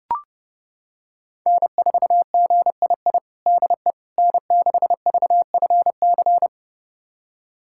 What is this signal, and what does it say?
Morse code, heard as a keyed tone, about 30 wpm, 700 Hz. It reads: N4GII DE N6VFC
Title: USA Callsign DE Callsign - 5 Characters - 30wpm